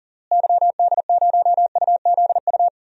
Morse code, heard as a keyed tone, about 40 wpm, 700 Hz. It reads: YD0UZU
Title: DXCC-100 Call Signs - 6 Characters - Repeated 3x - 40wpm